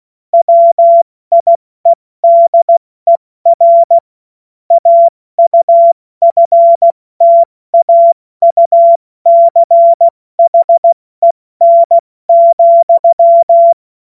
Morse code, heard as a keyed tone, about 16 wpm, 680 Hz. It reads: WIEDER AUFTAUCHEN,
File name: Morsekurs / Witze / Witz 01. August 2023 70 F80